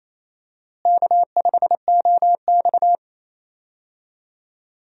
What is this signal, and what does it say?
Morse code, heard as a keyed tone, about 28 wpm, 700 Hz. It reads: K5OX
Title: USA Call Signs - 4 Characters - Repeated 3x - 28wpm